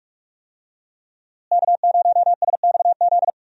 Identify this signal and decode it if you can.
Morse code, heard as a keyed tone, about 44 wpm, 700 Hz. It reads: K0SXZ
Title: USA Call Signs - 5 Characters - Repeated 3x - 45wpm